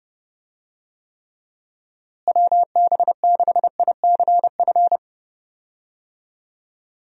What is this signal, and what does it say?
Morse code, heard as a keyed tone, about 30 wpm, 700 Hz. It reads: WB6ICF
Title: USA Call Signs - 6 Characters - Repeated 2x - 30wpm